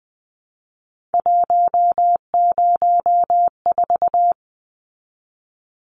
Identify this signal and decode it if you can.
Morse code, heard as a keyed tone, about 20 wpm, 700 Hz. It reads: 104
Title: Numbers - 3-Digit 20wpm